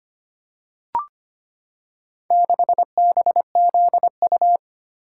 Morse code, heard as a keyed tone, about 25 wpm, 700 Hz. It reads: 6BZU
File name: practice / 25wpm / Four Letter-Number Combinations - 25wpm